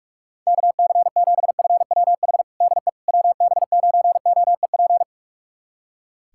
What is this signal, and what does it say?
Morse code, heard as a keyed tone, about 41 wpm, 690 Hz. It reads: KC7FWH DE WB9OEP
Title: USA Callsign DE Callsign - 6 Characters - 45wpm